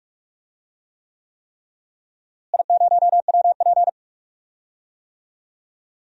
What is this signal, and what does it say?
Morse code, heard as a keyed tone, about 42 wpm, 700 Hz. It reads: I0WP
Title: DXCC-100 Call Signs - 4 Characters - Repeated 3x - 45wpm